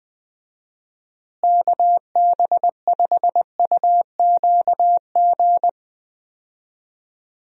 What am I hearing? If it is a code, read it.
Morse code, heard as a keyed tone, about 21 wpm, 700 Hz. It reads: KB5UQG
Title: USA Call Signs - 6 Characters - Repeated 2x - 20wpm